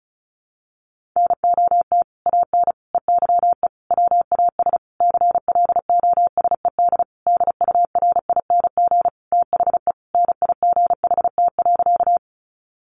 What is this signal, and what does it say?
Morse code, heard as a keyed tone, about 34 wpm, 700 Hz. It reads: NOT AN EYE WAS CLOSED DURING THE NIGHT.
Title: Sentences from Top 1400 Words 35wpm